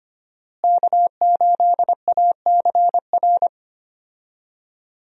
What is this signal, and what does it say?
Morse code, heard as a keyed tone, about 25 wpm, 700 Hz. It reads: K8ACR
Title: USA Call Signs - 5 Characters - Repeated 2x - 25wpm